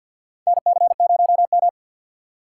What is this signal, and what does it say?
Morse code, heard as a keyed tone, about 53 wpm, 700 Hz. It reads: NC0M
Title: USA Call Signs - 4 Characters - 6x Character Spacing - 50wpm